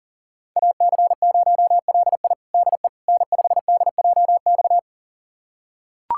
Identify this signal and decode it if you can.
Morse code, heard as a keyed tone, about 40 wpm, 700 Hz. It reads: AC0LI DE N5DJX
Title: USA Callsign DE Callsign 40wpm